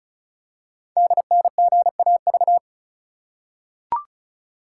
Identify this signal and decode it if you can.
Morse code, heard as a keyed tone, about 36 wpm, 690 Hz. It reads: DNGAV
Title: Code Groups - 5 Letters - 35wpm